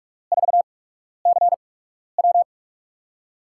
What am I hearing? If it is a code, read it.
Morse code, heard as a keyed tone, about 44 wpm, spaced as for 12 wpm, 700 Hz. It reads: 4CW
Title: USA Call Signs - 5 Characters - 8x Character Spacing - 45wpm